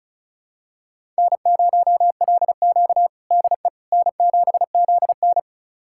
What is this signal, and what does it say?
Morse code, heard as a keyed tone, about 36 wpm, 700 Hz. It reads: N0LQ DE N7ZN
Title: USA Callsign DE Callsign - 4 Characters - 35wpm